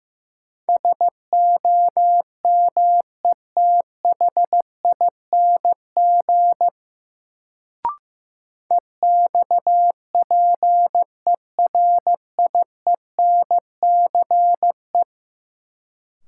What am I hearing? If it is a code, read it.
Morse code, heard as a keyed tone, about 15 wpm, 700 Hz. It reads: SOMETHING EXPERIENCE
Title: Sentences from Top 600 Words - New Word Drill - 15wpm